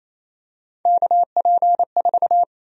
Morse code, heard as a keyed tone, about 28 wpm, 700 Hz. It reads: KP4
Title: USA Call Sign Prefixes - 28wpm